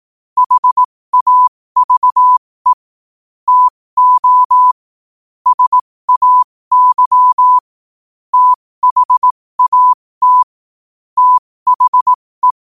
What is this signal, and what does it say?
Morse code, heard as a keyed tone, about 18 wpm, spaced as for 15 wpm, 1000 Hz. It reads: HAVE TO SAY THAT THE
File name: qotdfiles / 15wpm